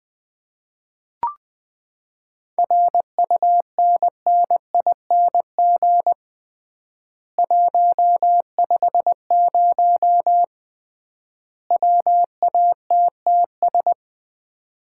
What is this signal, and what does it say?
Morse code, heard as a keyed tone, about 20 wpm, 700 Hz. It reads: RUNNING 150 WATTS
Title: Common QSO Phrases - 3x Word-Spacing - 20wpm